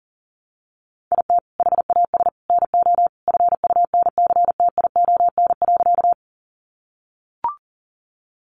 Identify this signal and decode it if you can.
Morse code, heard as a keyed tone, about 40 wpm, 700 Hz. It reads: IT HAS NO FUNCTION.
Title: Sentences from Top 1100 Words 40wpm